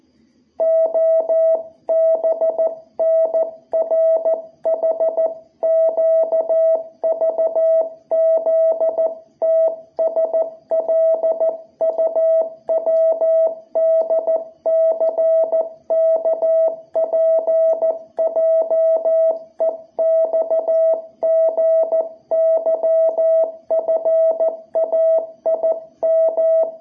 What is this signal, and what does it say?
Morse code, heard as a keyed tone, about 15 wpm, 660 Hz. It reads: OBNRHQVZTSLUWDCKPJEXGYFAIM
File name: jh8cbh / 2021cw